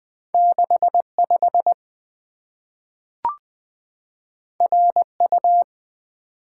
Morse code, heard as a keyed tone, about 20 wpm, 700 Hz. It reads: 65 RU
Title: Two Letter-Number Combinations - 20wpm